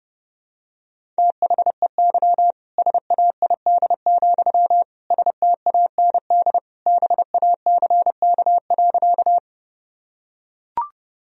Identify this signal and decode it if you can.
Morse code, heard as a keyed tone, about 30 wpm, 700 Hz. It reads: THEY SAID, STAND BACK.
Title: Sentences from Top 1000 Words - Review Part 1 - 30wpm